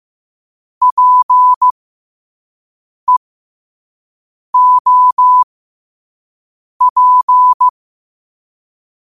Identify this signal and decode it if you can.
Morse code, heard as a keyed tone, about 15 wpm, spaced as for 5 wpm, 1000 Hz. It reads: PEOP
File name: qotdfiles / 5wpm